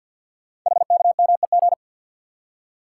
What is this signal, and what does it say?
Morse code, heard as a keyed tone, about 46 wpm, 700 Hz. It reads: HKMEG
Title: Code Groups - 5 Letters - 50wpm